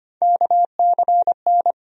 Morse code, heard as a keyed tone, about 25 wpm, 700 Hz. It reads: KCN
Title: USA Call Sign Postfixes - 3 Letters 25wpm